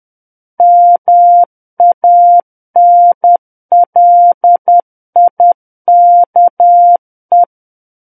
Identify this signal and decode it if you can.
Morse code, heard as a keyed tone, about 10 wpm, 700 Hz. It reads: MANLIKE
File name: Book One / 01-Chapter 1 - 10 WPM